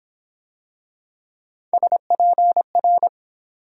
Morse code, heard as a keyed tone, about 25 wpm, 700 Hz. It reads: SPR